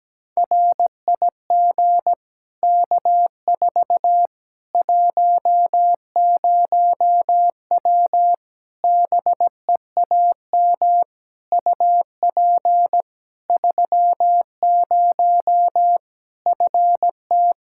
Morse code, heard as a keyed tone, about 17 wpm, 700 Hz. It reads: RIG K4 10W BEAM UP 30 FT
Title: K1USN SST QSO Practice 17wpm